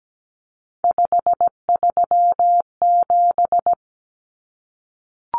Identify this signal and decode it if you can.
Morse code, heard as a keyed tone, about 17 wpm, 700 Hz. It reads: 537
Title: Numbers - 3-Digit 17wpm